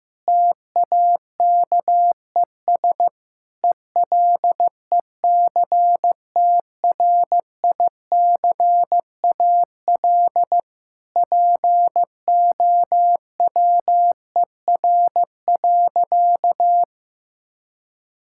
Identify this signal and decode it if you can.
Morse code, heard as a keyed tone, about 15 wpm, 700 Hz. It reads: TAKES ELECTRICAL POWER.
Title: Sentences from Top 1000 Words - Review Part 6 - 15wpm